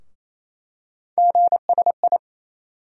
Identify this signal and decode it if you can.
Morse code, heard as a keyed tone, about 28 wpm, 700 Hz. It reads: GSI
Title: USA Call Sign Postfixes - 3 Letters 28wpm